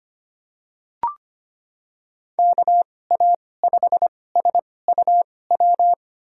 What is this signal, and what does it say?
Morse code, heard as a keyed tone, about 25 wpm, spaced as for 18 wpm, 700 Hz. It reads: KA5SUW
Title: USA Call Signs - 2x Character Spacing - 25wpm